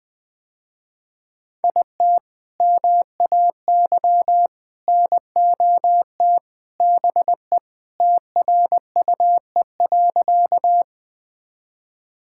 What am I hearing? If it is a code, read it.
Morse code, heard as a keyed tone, about 20 wpm, 700 Hz. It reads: IT MAY NOT BE TRUE.
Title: Sentences with 3-Word Phrases - Spoken Beginning - 20wpm